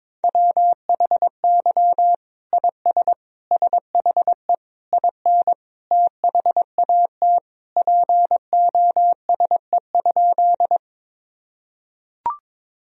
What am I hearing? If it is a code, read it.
Morse code, heard as a keyed tone, about 22 wpm, 700 Hz. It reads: WHY IS SHE IN THAT POSE?
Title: Sentences from Top 1000 Words - Review Part 2 - 22wpm